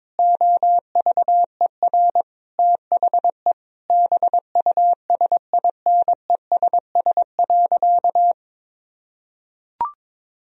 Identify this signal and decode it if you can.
Morse code, heard as a keyed tone, about 22 wpm, 700 Hz. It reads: OVER THE BUSINESS.
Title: Sentences from Top 1000 Words - Review Part 3 - 22wpm